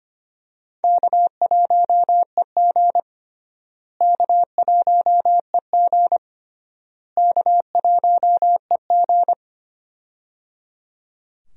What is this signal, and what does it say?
Morse code, heard as a keyed tone, about 25 wpm, 700 Hz. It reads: K1EG K1EG K1EG
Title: USA Call Signs - 4 Characters - Repeated 3x - 25wpm